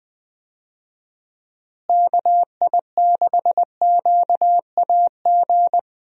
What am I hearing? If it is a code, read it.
Morse code, heard as a keyed tone, about 21 wpm, 700 Hz. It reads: KI6QAG
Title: USA Call Signs - 6 Characters - Repeated 3x - 20wpm